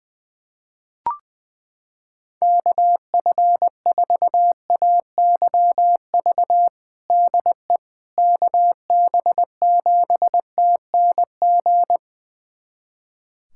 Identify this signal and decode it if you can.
Morse code, heard as a keyed tone, about 20 wpm, 700 Hz. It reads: KF4AYV DE KB7TNG
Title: USA Callsign DE Callsign - 6 Characters - 20wpm